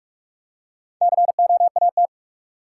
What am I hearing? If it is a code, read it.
Morse code, heard as a keyed tone, about 46 wpm, 700 Hz. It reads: COAT
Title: Four Letter Words - 45wpm